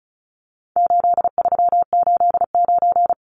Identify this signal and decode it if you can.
Morse code, heard as a keyed tone, about 34 wpm, 700 Hz. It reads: 8389
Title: Numbers - 4-Digit 35wpm